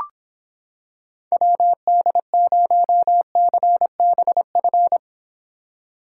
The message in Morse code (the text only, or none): WD0CBF